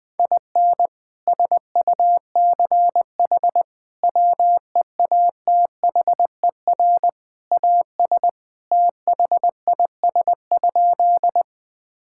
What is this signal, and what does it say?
Morse code, heard as a keyed tone, about 21 wpm, 700 Hz. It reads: IN SUCH WEATHER AS THIS?
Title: Sentences from Top 1000 Words - Review Part 5 - 20wpm